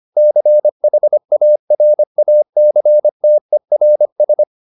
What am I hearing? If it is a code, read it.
Morse code, heard as a keyed tone, about 25 wpm, 600 Hz. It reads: CHARACTERS
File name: SS220 25